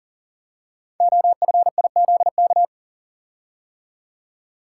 Morse code, heard as a keyed tone, about 41 wpm, 690 Hz. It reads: OFIZK
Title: Code Groups - 5 Letters - 40wpm